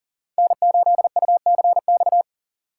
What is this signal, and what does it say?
Morse code, heard as a keyed tone, about 40 wpm, 700 Hz. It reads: N8UCX
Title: USA Call Signs - 6x Character Spacing - 40wpm